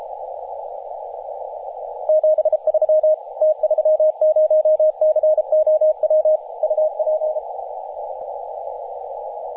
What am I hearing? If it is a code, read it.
Morse code, heard as a keyed tone, about 34 wpm, 620 Hz, with radio noise. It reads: 73 T30COW UP
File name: t30cow-18-072mhz-cw